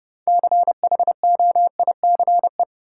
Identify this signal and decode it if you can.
Morse code, heard as a keyed tone, about 30 wpm, 700 Hz. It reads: CHOICE
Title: Six Letter Words - 30wpm